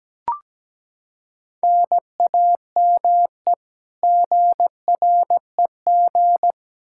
Morse code, heard as a keyed tone, about 17 wpm, 700 Hz. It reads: NAME GREG
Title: Practice QSO - Exchange 1 - Beginning - 17wpm